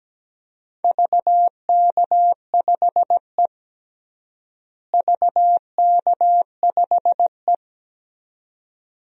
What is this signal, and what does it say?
Morse code, heard as a keyed tone, about 17 wpm, 700 Hz. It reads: VK5E VK5E
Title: DXCC-100 Call Signs - 4 Characters - Repeated 2x - 17wpm